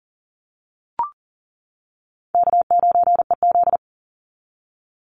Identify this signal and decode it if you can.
Morse code, heard as a keyed tone, about 41 wpm, 700 Hz. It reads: K9EZ
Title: NA SOTA Chaser Callsigns 40wpm